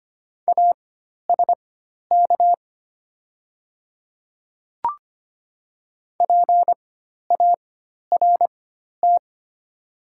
Morse code, heard as a keyed tone, about 25 wpm, spaced as for 12 wpm, 690 Hz. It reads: ASK PART
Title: 4x Character Spacing - Top 200 Words - 25wpm